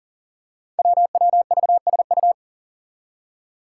Morse code, heard as a keyed tone, about 40 wpm, 700 Hz. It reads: WWVSU
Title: Five Challenging Letter-Number Combinations 40wpm